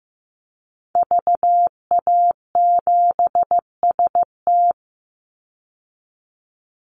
Morse code, heard as a keyed tone, about 15 wpm, 700 Hz. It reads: VA7ST
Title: Call Signs Repeated 3x - 15wpm